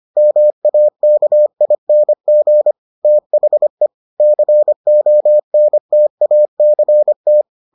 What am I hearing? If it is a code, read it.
Morse code, heard as a keyed tone, about 25 wpm, 600 Hz. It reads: MAKING THE CONTACT